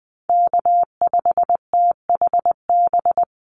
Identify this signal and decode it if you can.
Morse code, heard as a keyed tone, about 20 wpm, 700 Hz. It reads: K5THB
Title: USA Call Signs 20wpm